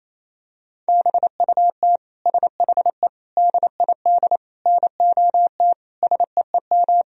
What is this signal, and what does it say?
Morse code, heard as a keyed tone, about 28 wpm, 700 Hz. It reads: BUT SHE DID NOT SEEM
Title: Sentences from Top 1000 Words - Review Part 1 - 28wpm